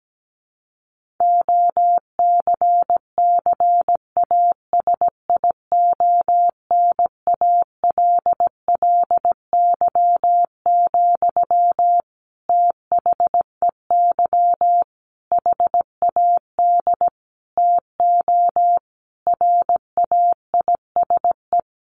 Morse code, heard as a keyed tone, about 17 wpm, 700 Hz. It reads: OCCASIONALLY, THEY HAD TO RAISE
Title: Sentences from Top 1400 Words 17wpm